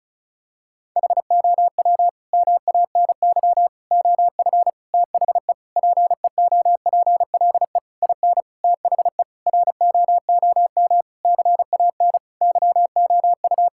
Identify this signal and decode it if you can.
Morse code, heard as a keyed tone, about 36 wpm, 700 Hz. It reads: HOW MANY OF THE PEOPLE IN THE ROOM CAN YOU
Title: Sentences from Top 1000 Words - Review Part 6 - 35wpm